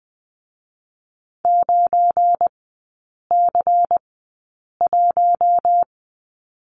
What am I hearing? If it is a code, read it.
Morse code, heard as a keyed tone, about 20 wpm, spaced as for 8 wpm, 700 Hz. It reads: 9C1
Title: 172 Lesson 28e - Number 9 - Three Characters 20wpm